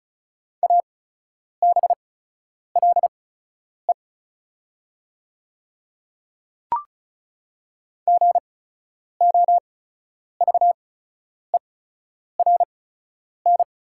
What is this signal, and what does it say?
Morse code, heard as a keyed tone, about 36 wpm, spaced as for 10 wpm, 700 Hz. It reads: ABLE GOVERN
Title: 8x Character Spacing - Top 500 Words - 35wpm